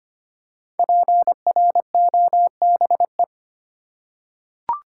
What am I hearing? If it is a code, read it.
Morse code, heard as a keyed tone, about 25 wpm, 700 Hz. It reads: PROBE
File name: 5 Letter Words - First 3 Letters Given - 25wpm